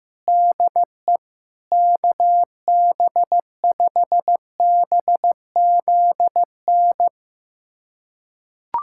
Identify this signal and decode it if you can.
Morse code, heard as a keyed tone, about 15 wpm, 700 Hz. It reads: DE KB5BZN
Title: USA Callsign DE Callsign - 6 Characters - 15wpm